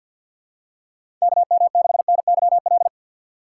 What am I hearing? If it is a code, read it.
Morse code, heard as a keyed tone, about 49 wpm, 700 Hz. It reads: KM6NYL
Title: USA Call Signs - 6 Characters - 6x Character Spacing - 50wpm